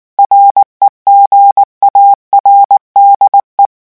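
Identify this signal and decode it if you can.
Morse code, heard as a keyed tone, about 19 wpm, 800 Hz. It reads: REGARDE
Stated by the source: French Morse revision 8 drill of random five-letter groups at 18 words per minute.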